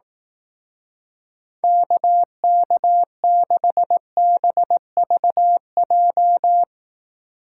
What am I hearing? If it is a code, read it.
Morse code, heard as a keyed tone, about 18 wpm, 700 Hz. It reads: KK6BVJ
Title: Speed Racing - USA 2x3 Call Signs 15wpm